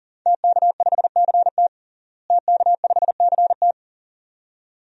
Morse code, heard as a keyed tone, about 40 wpm, 700 Hz. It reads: TK5CT TK5CT
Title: DXCC-100 Call Signs - 5 Characters - Repeated 2x - 40wpm